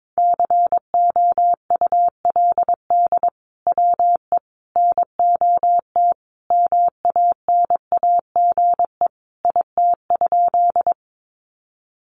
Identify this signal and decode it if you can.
Morse code, heard as a keyed tone, about 22 wpm, 700 Hz. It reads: COULD WE NOT MANAGE IT?
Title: Sentences from Top 1400 Words 22wpm